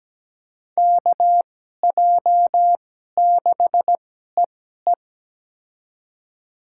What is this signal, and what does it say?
Morse code, heard as a keyed tone, about 17 wpm, spaced as for 12 wpm, 700 Hz. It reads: KJ6EE
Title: USA Call Signs - 5 Characters - 2x Character Spacing - 17wpm